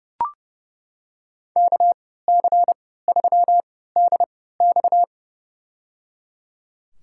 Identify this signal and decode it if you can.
Morse code, heard as a keyed tone, about 29 wpm, spaced as for 17 wpm, 700 Hz. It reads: KC3DX
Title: USA Call Signs - 5 Characters - 3x Character Spacing - 30wpm